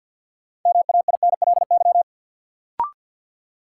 Morse code, heard as a keyed tone, about 50 wpm, 700 Hz. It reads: MAINLY
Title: Sentences from Top 1400 Words - New Word Drill Alt - 50wpm